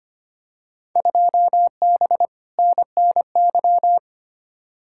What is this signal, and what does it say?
Morse code, heard as a keyed tone, about 25 wpm, 700 Hz. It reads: 2B NNY
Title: Field Day Exchanges Encore - 25wpm